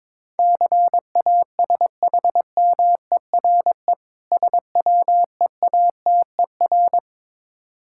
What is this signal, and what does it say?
Morse code, heard as a keyed tone, about 22 wpm, 690 Hz. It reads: CASHMERE SWEATER
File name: Difficult Words with Hints - 22wpm